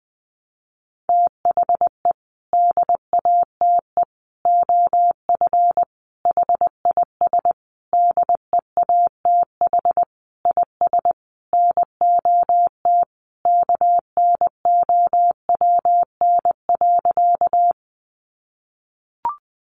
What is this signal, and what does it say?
Morse code, heard as a keyed tone, about 20 wpm, 700 Hz. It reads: THE DATE OF HIS DEATH IS NOT KNOWN.
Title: Sentences from Top 1100 Words 20wpm